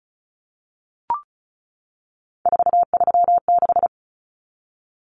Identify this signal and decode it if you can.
Morse code, heard as a keyed tone, about 34 wpm, 700 Hz. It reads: 436
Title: Numbers - 3-Digit 35wpm